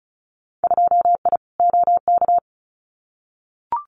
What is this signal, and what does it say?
Morse code, heard as a keyed tone, about 34 wpm, 700 Hz. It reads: 2I OK
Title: Winter Field Day Exchanges Encore 35wpm